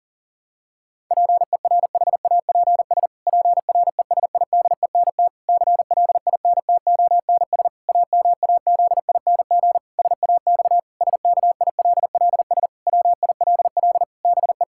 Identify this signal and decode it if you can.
Morse code, heard as a keyed tone, about 40 wpm, 700 Hz. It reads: PERHAPS PRESIDENT CLINTONS AMAZING SAX SKILLS WILL BE
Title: Pangram Challenge v2 - 40wpm